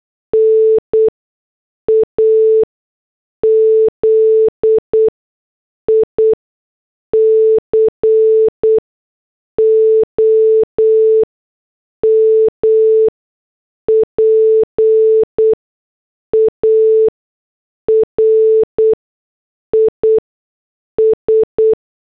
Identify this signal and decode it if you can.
Morse code, heard as a keyed tone, about 8 wpm, 440 Hz. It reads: N A Z I C O M P A R I S
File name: morse / headlines